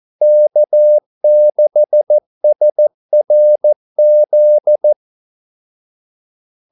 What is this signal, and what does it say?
Morse code, heard as a keyed tone, about 14 wpm, 600 Hz. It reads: K6SRZ
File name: Callsigns-14wpm